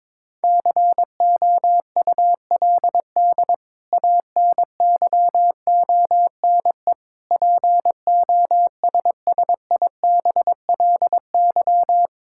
Morse code, heard as a keyed tone, about 22 wpm, 700 Hz. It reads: COULD ANYONE POSSIBLY
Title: Sentences from Top 1600 Words 22wpm